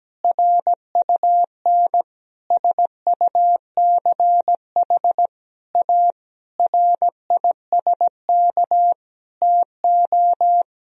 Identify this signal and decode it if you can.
Morse code, heard as a keyed tone, about 17 wpm, 700 Hz. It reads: RUN SUCH A RISK TO
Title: Sentences from Top 1000 Words - Review Part 4 - 17wpm